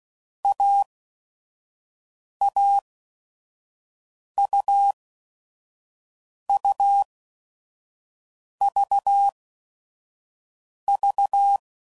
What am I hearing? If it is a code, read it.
Morse code, heard as a keyed tone, about 16 wpm, spaced as for 5 wpm, 800 Hz. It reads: AAUUVV
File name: Combinations / Difficult Combos / AUV AUV4